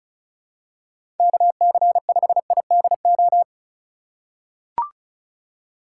Morse code, heard as a keyed tone, about 34 wpm, 700 Hz. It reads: KC5IDO
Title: USA Call Signs - 6 Characters - 2x Character Spacing - 35wpm